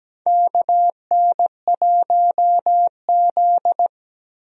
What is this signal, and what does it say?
Morse code, heard as a keyed tone, about 17 wpm, 700 Hz. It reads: KN1Z
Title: USA Call Signs - 4 Characters - Repeated 3x - 17wpm